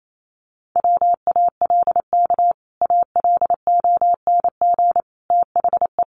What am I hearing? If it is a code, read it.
Morse code, heard as a keyed tone, about 28 wpm, 700 Hz. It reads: WALK ALONG THE
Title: Sentences from Top 1100 Words 28wpm